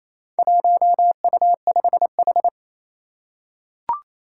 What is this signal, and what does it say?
Morse code, heard as a keyed tone, about 28 wpm, 700 Hz. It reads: 1U5H
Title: Four Challenging Letter-Number Combinations 28wpm